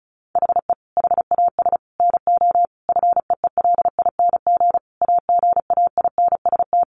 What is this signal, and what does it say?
Morse code, heard as a keyed tone, about 34 wpm, 700 Hz. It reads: HE HAS NO FEELING AGAINST
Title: Sentences from Top 1200 Words 35wpm